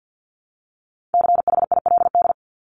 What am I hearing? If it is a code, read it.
Morse code, heard as a keyed tone, about 50 wpm, 700 Hz. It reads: CHILD
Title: Top 501-750 Words 50wpm